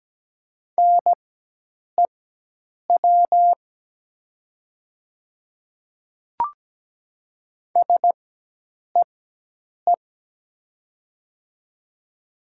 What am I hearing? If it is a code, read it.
Morse code, heard as a keyed tone, about 17 wpm, spaced as for 8 wpm, 700 Hz. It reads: NEW SEE
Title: 4x Character Spacing - Top 200 Words - 17wpm